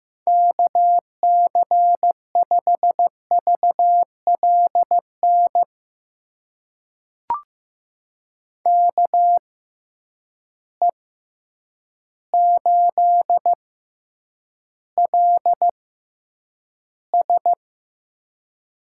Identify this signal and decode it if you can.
Morse code, heard as a keyed tone, about 15 wpm, spaced as for 8 wpm, 700 Hz. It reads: KC5VLN KE8LS
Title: USA Call Signs - 6 Characters - 6x Character Spacing - 15wpm